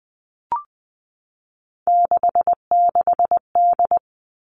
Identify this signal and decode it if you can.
Morse code, heard as a keyed tone, about 20 wpm, 700 Hz. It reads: 66D
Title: All Cut Numbers - 3-Digit 20wpm